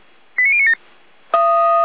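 Five-tone selective call (ZVEI), heard as quick runs of five tones, tones of about 70 ms. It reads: 89097
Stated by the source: ZVEI five-tone call 89023 received on SDR.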